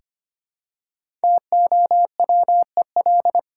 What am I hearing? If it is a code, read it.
Morse code, heard as a keyed tone, about 25 wpm, 700 Hz. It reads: TOWEL
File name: practice / 25wpm / Easy Words with Hints - 25wpm